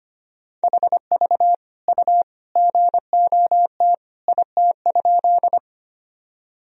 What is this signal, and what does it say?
Morse code, heard as a keyed tone, about 25 wpm, 700 Hz. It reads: HV U GOT IT?
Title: Sentences from Top 300 Words and Abbreviations 25wpm